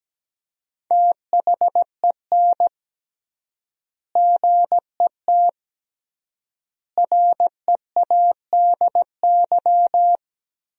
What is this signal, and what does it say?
Morse code, heard as a keyed tone, about 17 wpm, 700 Hz. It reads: THEN GET READY
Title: 3x Word-Spacing - Sentences from Top 400 Words - 17wpm